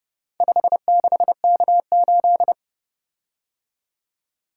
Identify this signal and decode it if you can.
Morse code, heard as a keyed tone, about 29 wpm, 700 Hz. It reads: 56K8